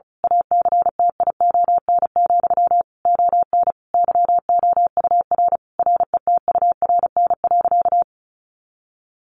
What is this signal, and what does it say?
Morse code, heard as a keyed tone, about 34 wpm, 700 Hz. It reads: ACTION, ON YOUR RETURN.
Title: A Sound Of Thunder 35wpm